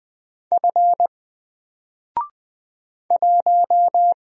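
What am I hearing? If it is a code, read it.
Morse code, heard as a keyed tone, about 20 wpm, 700 Hz. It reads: F 1
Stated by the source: Single letter and number Morse drill at 20 wpm, from N4LCD.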